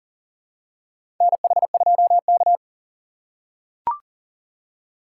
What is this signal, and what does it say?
Morse code, heard as a keyed tone, about 40 wpm, 700 Hz. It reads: NH2K
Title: USA Call Signs - 4 Characters - 4x Character Spacing - 40wpm